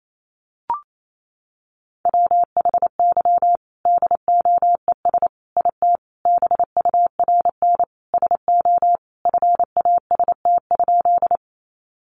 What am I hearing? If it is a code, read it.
Morse code, heard as a keyed tone, about 28 wpm, 700 Hz. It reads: WHY DOES IT BURN SO FAST?
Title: Sentences from Top 1300 Words 28wpm